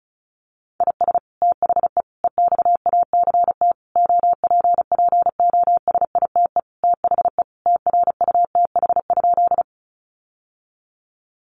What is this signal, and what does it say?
Morse code, heard as a keyed tone, about 34 wpm, 700 Hz. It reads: IS THE EXACT OPPOSITE THE TRUTH?
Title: Sentences from Top 1300 Words 35wpm